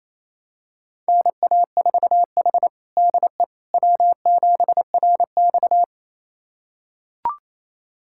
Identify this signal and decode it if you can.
Morse code, heard as a keyed tone, about 27 wpm, 700 Hz. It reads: NA4H DE W7RX
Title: USA Callsign DE Callsign - 4 Characters - 28wpm